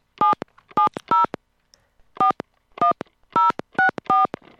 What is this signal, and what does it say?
DTMF, heard as telephone touch tones, tones of about 130 ms. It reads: 77041064